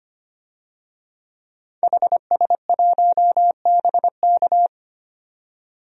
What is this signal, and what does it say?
Morse code, heard as a keyed tone, about 25 wpm, 700 Hz. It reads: HS1BK
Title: DXCC-100 Call Signs - 5 Characters - 25wpm